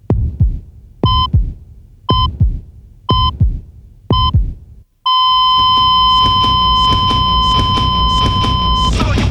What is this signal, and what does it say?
There is no code.